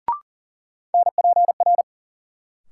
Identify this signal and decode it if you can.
Morse code, heard as a keyed tone, about 40 wpm, 700 Hz. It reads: NPR
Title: Sets of 5 Words - New Word Drill - 40wpm